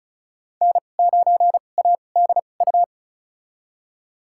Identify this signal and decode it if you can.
Morse code, heard as a keyed tone, about 36 wpm, spaced as for 25 wpm, 700 Hz. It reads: N9ADU
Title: USA Call Signs - 5 Characters - 2x Character Spacing - 35wpm